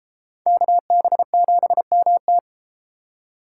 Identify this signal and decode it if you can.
Morse code, heard as a keyed tone, about 33 wpm, 700 Hz. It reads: KB7MT